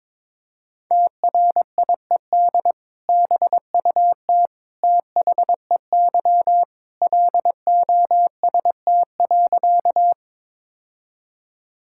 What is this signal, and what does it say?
Morse code, heard as a keyed tone, about 22 wpm, 700 Hz. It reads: TRIED BUT THEY LOST.
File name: practice / 22wpm / Sentences with 2-Word Phrases - 22wpm